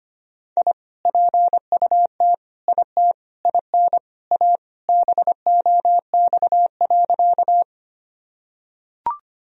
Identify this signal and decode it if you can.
Morse code, heard as a keyed tone, about 25 wpm, 700 Hz. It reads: I PUT IT IN A BOX.
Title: Sentences from Top 1000 Words - Review Part 2 - 25wpm